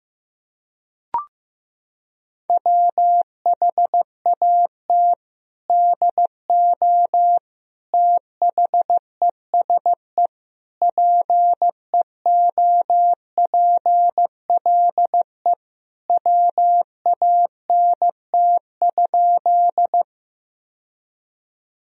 Morse code, heard as a keyed tone, about 16 wpm, 700 Hz. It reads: WHAT DO THESE PEOPLE WANT?
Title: Sentences from Top 1000 Words - Review Part 1 - 15wpm